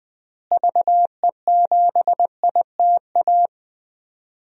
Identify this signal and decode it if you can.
Morse code, heard as a keyed tone, about 21 wpm, 700 Hz. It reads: VE7ITA
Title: DXCC-100 Call Signs - 6 Characters - Repeated 2x - 20wpm